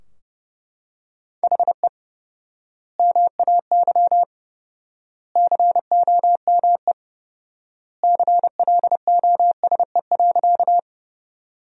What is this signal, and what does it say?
Morse code, heard as a keyed tone, about 32 wpm, 700 Hz. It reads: HE MAY COME CLOSE.
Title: 4x Word-Spacing - Sentences from Top 300 Words - 30wpm